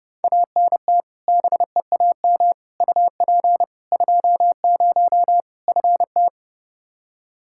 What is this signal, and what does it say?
Morse code, heard as a keyed tone, about 32 wpm, 700 Hz. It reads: ANT BEAM UP 20 FT
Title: Practice QSO - Exchange 2 - Info - 30wpm